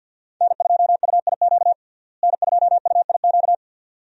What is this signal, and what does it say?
Morse code, heard as a keyed tone, about 50 wpm, 700 Hz. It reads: N2UIQ N2UIQ
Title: USA Call Signs - 5 Characters - Repeated 2x - 50wpm